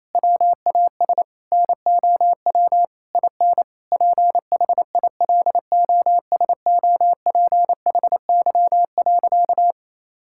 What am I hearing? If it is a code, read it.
Morse code, heard as a keyed tone, about 28 wpm, 700 Hz. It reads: WAS NOW IN PHILOSOPHY.
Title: Sentences from Top 1000 Words - Review Part 6 - 28wpm